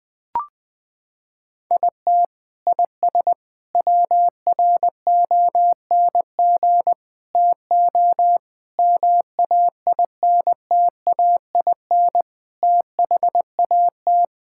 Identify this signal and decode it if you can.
Morse code, heard as a keyed tone, about 21 wpm, 700 Hz. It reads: IT IS WRONG TO MAINTAIN THAT
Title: Sentences from Top 1000 Words - Review Part 5 - 20wpm